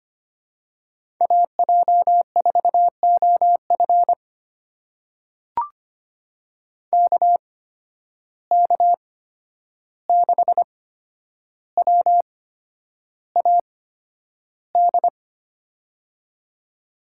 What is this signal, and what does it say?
Morse code, heard as a keyed tone, about 25 wpm, spaced as for 10 wpm, 700 Hz. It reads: AJ4OF KK6WAD
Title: USA Call Signs - 8x Character Spacing - 25wpm